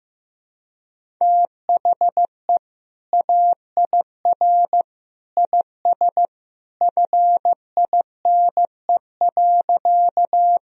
Morse code, heard as a keyed tone, about 15 wpm, 700 Hz. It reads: THE AIR IS FINE.
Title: Sentences from Top 1000 Words - Review Part 1 - 15wpm